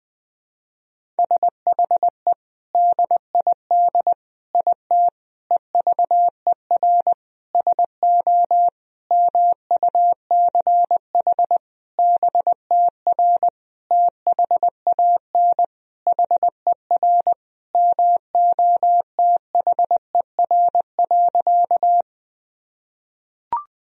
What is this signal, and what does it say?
Morse code, heard as a keyed tone, about 21 wpm, 700 Hz. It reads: SHE DID IT EVER SO MUCH BTR THAN HER MOTHER.
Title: Sentences from Top 400 Words and Abbreviations 20wpm